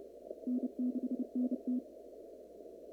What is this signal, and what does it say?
Morse code, heard as a keyed tone, about 30 wpm, 250 Hz. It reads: N6NT
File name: CW / kt5j wpx cw-op-k5pi